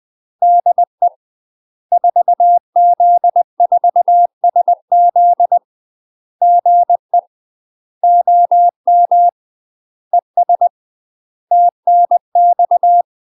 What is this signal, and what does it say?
Morse code, heard as a keyed tone, about 20 wpm, 700 Hz. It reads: DE 4Z4SZ GE OM ES TNX
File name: UK MASTER QSOs TRAINING / 20WPM / UK MASTER QSOs 0034